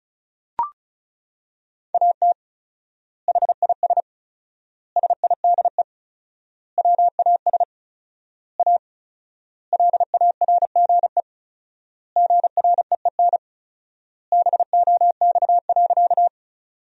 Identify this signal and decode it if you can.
Morse code, heard as a keyed tone, about 36 wpm, 700 Hz. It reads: AT HIS SIDE WAS A LARGE GREEN BOX.
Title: 4x Word-Spacing - Sentences from Top 500 Words - 35wpm